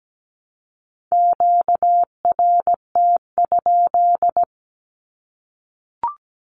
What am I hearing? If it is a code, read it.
Morse code, heard as a keyed tone, about 17 wpm, 700 Hz. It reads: QRT?
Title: Q-Codes Repeated 3x 17wpm